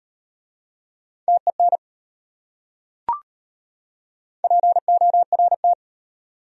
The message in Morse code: TEN PORT

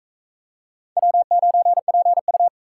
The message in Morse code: W9PU

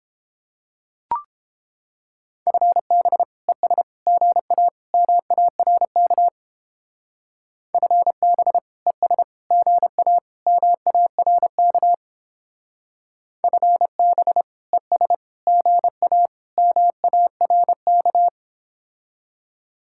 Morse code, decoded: FB ES GA MARK FB ES GA MARK FB ES GA MARK

